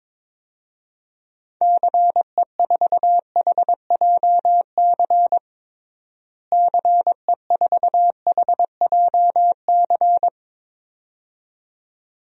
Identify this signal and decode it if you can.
Morse code, heard as a keyed tone, about 22 wpm, 700 Hz. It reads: CE4HJC CE4HJC
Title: DXCC-100 Call Signs - 6 Characters - Repeated 2x - 22wpm